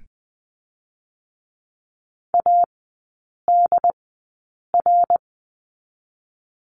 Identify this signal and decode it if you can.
Morse code, heard as a keyed tone, about 20 wpm, spaced as for 8 wpm, 700 Hz. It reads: ADR
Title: 067 Lesson 12e - Letter d - Three Characters 20wpm